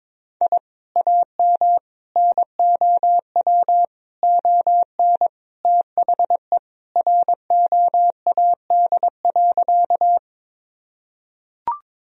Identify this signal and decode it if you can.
Morse code, heard as a keyed tone, about 22 wpm, 700 Hz. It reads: I AM NOW ON THE ROAD.
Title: Sentences from Top 1000 Words - Review Part 2 - 22wpm